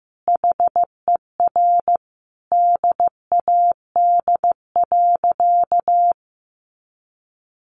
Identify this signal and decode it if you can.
Morse code, heard as a keyed tone, about 15 wpm, 700 Hz. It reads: HER DAD.
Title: Additional Easy Phrases 15wpm